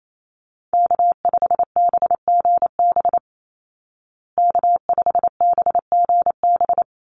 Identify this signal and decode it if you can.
Morse code, heard as a keyed tone, about 28 wpm, 690 Hz. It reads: K5BGB K5BGB
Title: Call Signs Repeated 3x - 28wpm